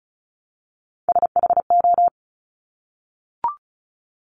Morse code, heard as a keyed tone, about 36 wpm, 700 Hz. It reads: SHO
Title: Top Three Letter Combinations in Words 35wpm